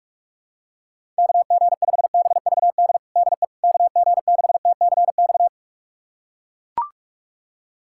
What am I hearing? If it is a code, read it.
Morse code, heard as a keyed tone, about 41 wpm, 700 Hz. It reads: KG5BVD DE KG6TCX
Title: USA Callsign DE Callsign 45wpm